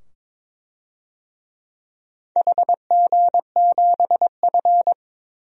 Morse code, heard as a keyed tone, about 22 wpm, 700 Hz. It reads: HG7F
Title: DXCC-100 Call Signs - 4 Characters - 22wpm